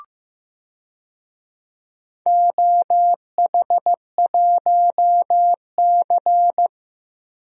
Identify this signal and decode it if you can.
Morse code, heard as a keyed tone, about 15 wpm, 700 Hz. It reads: OH1C
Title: DXCC-100 Call Signs - 4 Characters - Repeated 3x - 15wpm